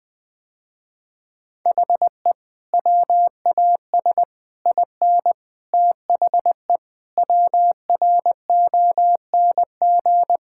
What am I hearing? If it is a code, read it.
Morse code, heard as a keyed tone, about 20 wpm, 700 Hz. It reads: HE WAS IN THE WRONG
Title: Sentences with 8-Word Phrases - Spoken Beginning - 20wpm